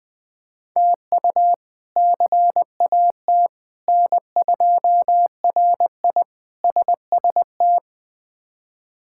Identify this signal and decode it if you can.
Morse code, heard as a keyed tone, about 20 wpm, 700 Hz. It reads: TU CAT N2RI SST
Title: K1USN SST QSO Practice - 20wpm